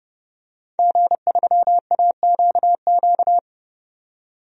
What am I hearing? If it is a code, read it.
Morse code, heard as a keyed tone, about 31 wpm, 700 Hz. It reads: G3AQQ
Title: DXCC-100 Call Signs - 5 Characters - 30wpm